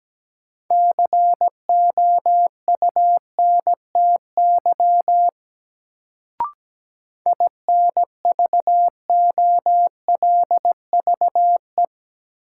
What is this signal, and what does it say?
Morse code, heard as a keyed tone, about 17 wpm, 700 Hz. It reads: COUNTY INVOLVE
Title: Sentences from Top 1200 Words - New Word Drill - 17wpm